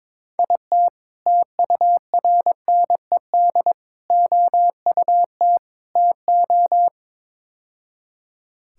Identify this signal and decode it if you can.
Morse code, heard as a keyed tone, about 22 wpm, 700 Hz. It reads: IT TURNED OUT TO
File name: practice / 22wpm / Build A Phrase - 5-Word Phrases - 22wpm